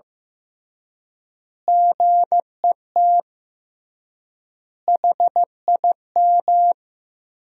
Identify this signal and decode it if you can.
Morse code, heard as a keyed tone, about 16 wpm, 700 Hz. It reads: GET HIM
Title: 3x Word-Spacing - Sentences from Top 400 Words - 15wpm